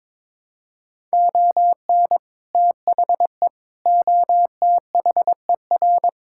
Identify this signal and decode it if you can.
Morse code, heard as a keyed tone, about 22 wpm, 700 Hz. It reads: ON THE OTHER
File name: practice / 22wpm / Sets of 7 Words - 22wpm